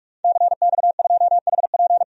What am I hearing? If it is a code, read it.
Morse code, heard as a keyed tone, about 46 wpm, 700 Hz. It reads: CX2HP
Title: DXCC-100 Call Signs - 5 Characters - 45wpm